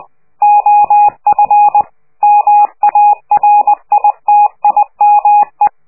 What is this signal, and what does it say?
Morse code, heard as a keyed tone, about 20 wpm, 830 Hz. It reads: OF MARITIME
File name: worldcoastal / scheveningenradio / morsepch